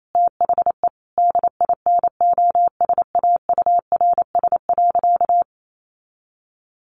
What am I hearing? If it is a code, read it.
Morse code, heard as a keyed tone, about 28 wpm, 700 Hz. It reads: THE DINOSAURS.
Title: A Sound Of Thunder 28wpm